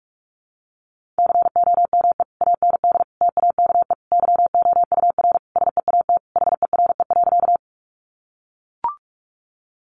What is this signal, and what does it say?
Morse code, heard as a keyed tone, about 44 wpm, 700 Hz. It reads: COME AND TAKE YOUR SEAT HERE.
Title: Sentences from Top 1500 Words 45wpm